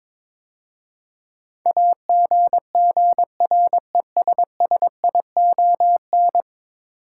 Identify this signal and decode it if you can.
Morse code, heard as a keyed tone, about 22 wpm, 700 Hz. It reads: AGGRESSION
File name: practice / 22wpm / 10 Letter Words - First 3 Letters Given - 22wpm